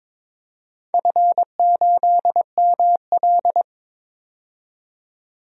Morse code, heard as a keyed tone, about 21 wpm, 700 Hz. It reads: F8ML
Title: DXCC-100 Call Signs - 4 Characters - 22wpm